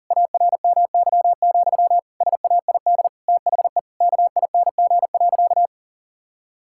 Morse code, heard as a keyed tone, about 37 wpm, 700 Hz. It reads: ARMY, SAID THE KING.
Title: Sentences from Top 1000 Words - Review Part 5 - 40wpm